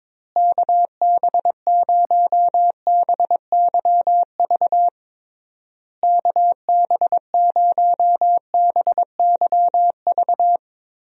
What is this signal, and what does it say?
Morse code, heard as a keyed tone, about 22 wpm, 700 Hz. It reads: KB0BYV KB0BYV
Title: USA Call Signs - 6 Characters - Repeated 3x - 22wpm